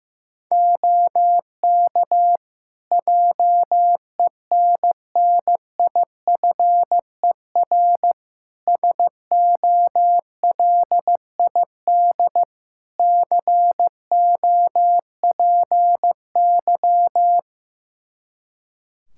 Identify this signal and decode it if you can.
Morse code, heard as a keyed tone, about 15 wpm, 700 Hz. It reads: OK JENNIFER SOLID COPY
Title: Practice QSO - Exchange 3 - More Info - 15wpm